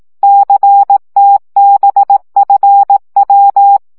Morse code, heard as a keyed tone, about 18 wpm, 800 Hz. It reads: CTBFW
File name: morze vizsga